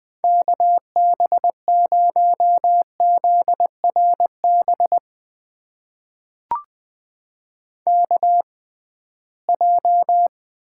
Morse code, heard as a keyed tone, about 20 wpm, spaced as for 15 wpm, 700 Hz. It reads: KB0ZRB KJ
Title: USA Call Signs - 6 Characters - 6x Character Spacing - 20wpm